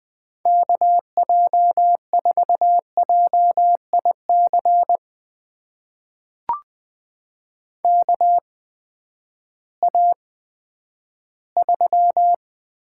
Morse code, heard as a keyed tone, about 20 wpm, spaced as for 11 wpm, 700 Hz. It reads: KJ4JIC KA3
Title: USA Call Signs - 6 Characters - 8x Character Spacing - 20wpm